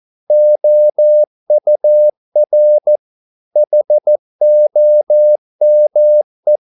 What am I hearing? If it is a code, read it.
Morse code, heard as a keyed tone, about 14 wpm, 600 Hz. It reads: OUR HOME